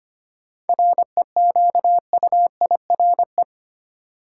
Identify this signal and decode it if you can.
Morse code, heard as a keyed tone, about 25 wpm, 700 Hz. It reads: REQUIRE
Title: Sentences from Top 900 Words - New Word Drill - 25wpm